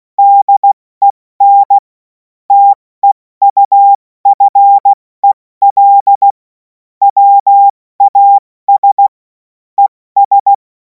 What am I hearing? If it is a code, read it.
Morse code, heard as a keyed tone, about 16 wpm, spaced as for 14 wpm, 800 Hz. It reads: DEN TEUFEL WAS ES
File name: Morsekurs / Witze / Witz 03. März 2026 70 F80